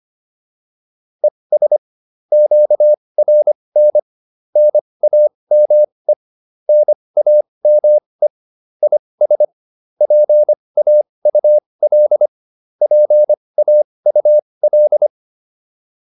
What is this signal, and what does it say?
Morse code, heard as a keyed tone, about 25 wpm, spaced as for 20 wpm, 600 Hz. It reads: ES QRN NAME NAME IS PAUL PAUL